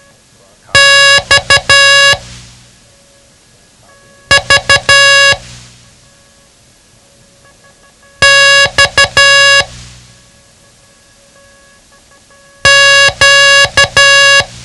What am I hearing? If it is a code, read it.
Morse code, heard as a keyed tone, about 12 wpm, 1700 Hz. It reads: X V X Q
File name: ham radio / CodeQuick / Code Quick 07